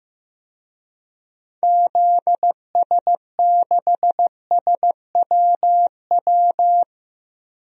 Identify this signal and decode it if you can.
Morse code, heard as a keyed tone, about 15 wpm, 700 Hz. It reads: ZS6SWW
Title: DXCC-100 Call Signs - 6 Characters - 15wpm